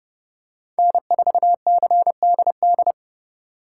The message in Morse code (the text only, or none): N4CDD